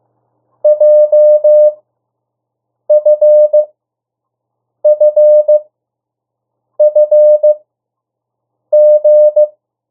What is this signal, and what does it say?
Morse code, heard as a keyed tone, about 15 wpm, spaced as for 6 wpm, 600 Hz. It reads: JFFFG
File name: urok 1